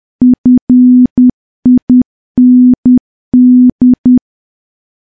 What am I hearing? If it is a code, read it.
Morse code, heard as a keyed tone, about 10 wpm, 250 Hz. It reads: FIND
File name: morse / morse-training-10WPM